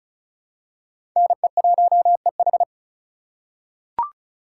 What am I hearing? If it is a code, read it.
Morse code, heard as a keyed tone, about 36 wpm, 700 Hz. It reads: NE1EH
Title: USA Call Signs - 3x Character Spacing - 35wpm